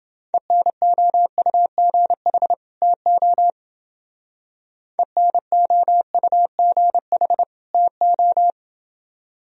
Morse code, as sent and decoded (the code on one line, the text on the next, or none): . -. --- ..- --. .... / - --- / . -. --- ..- --. .... / - ---
ENOUGH TO ENOUGH TO